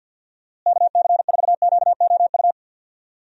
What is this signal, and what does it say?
Morse code, heard as a keyed tone, about 51 wpm, 700 Hz. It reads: KC4QOU
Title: USA Call Signs - 6 Characters - Repeated 2x - 50wpm